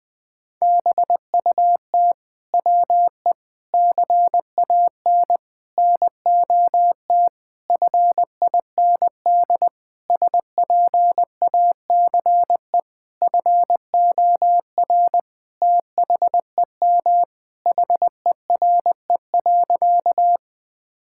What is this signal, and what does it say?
Morse code, heard as a keyed tone, about 19 wpm, 700 Hz. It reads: BUT WE CAN NOT FIND SPACE FOR THEM HERE.
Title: Sentences from Top 1000 Words - Review Part 5 - 20wpm